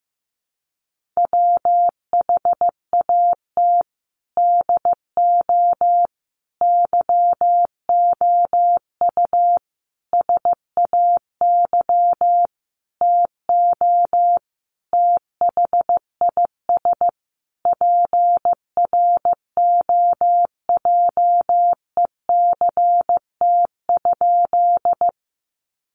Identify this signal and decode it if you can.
Morse code, heard as a keyed tone, about 15 wpm, 700 Hz. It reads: WHAT DO YOU SAY TO THIS PROJECT?
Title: Sentences from Top 1400 Words 15wpm